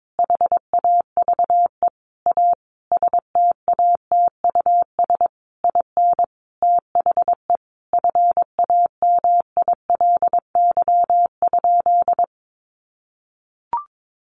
Morse code, heard as a keyed tone, about 22 wpm, 700 Hz. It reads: HAVE A STATUS IN THE FAMILY?
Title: Sentences from Top 1400 Words 22wpm